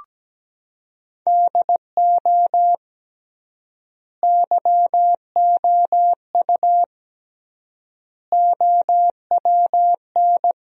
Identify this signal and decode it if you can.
Morse code, heard as a keyed tone, about 17 wpm, 700 Hz. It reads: DO YOU OWN
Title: 3x Word-Spacing - Sentences from Top 400 Words - 17wpm